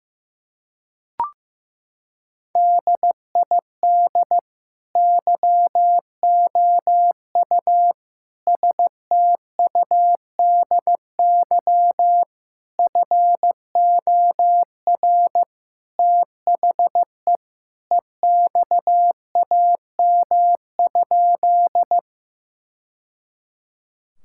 Morse code, heard as a keyed tone, about 15 wpm, 700 Hz. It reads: DID YOU STUDY FOR THE EXAM?
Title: Sentences from Top 1000 Words - Review Part 5 - 15wpm